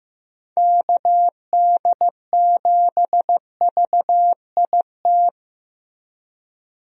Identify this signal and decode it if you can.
Morse code, heard as a keyed tone, about 15 wpm, 700 Hz. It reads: KD7VIT
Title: USA Call Signs - 6 Characters - Repeated 2x - 15wpm